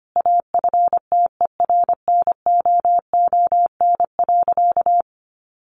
Morse code, heard as a keyed tone, about 25 wpm, 700 Hz. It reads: AFTERNOON.
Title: Sentences from Top 1500 Words 25wpm